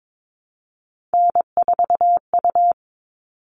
Morse code, heard as a keyed tone, about 22 wpm, 700 Hz. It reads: N4U
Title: All Cut Numbers - 3-Digit 22wpm